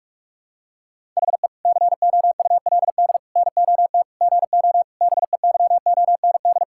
Morse code, heard as a keyed tone, about 46 wpm, 700 Hz. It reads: HE COULD NOT GO BEYOND
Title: Sentences from Top 1000 Words - Review Part 6 - 45wpm